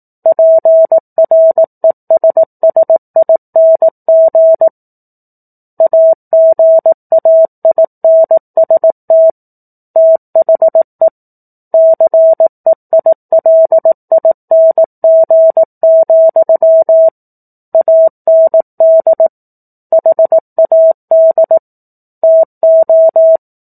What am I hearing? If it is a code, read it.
Morse code, heard as a keyed tone, about 18 wpm, 650 Hz. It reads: PRESSING AGAINST THE CEILING, AND HAD TO
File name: 04-Chapter-04 18 WPM